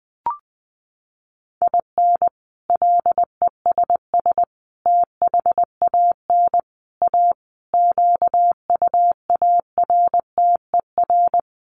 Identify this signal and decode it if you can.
Morse code, heard as a keyed tone, about 20 wpm, 700 Hz. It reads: IN LESS THAN A QUARTER
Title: Sentences from Top 1300 Words 20wpm